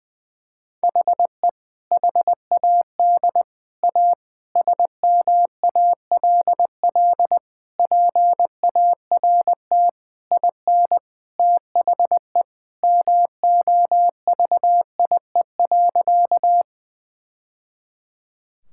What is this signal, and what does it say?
Morse code, heard as a keyed tone, about 21 wpm, 700 Hz. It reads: HE HAD A SMALL PART IN THE MOVIE.
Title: Sentences from Top 1000 Words - Review Part 5 - 20wpm